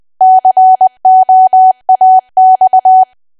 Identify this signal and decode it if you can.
Morse code, heard as a keyed tone, about 20 wpm, 750 Hz. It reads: COAX
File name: cwa / W1AW-Level3 / 20WPM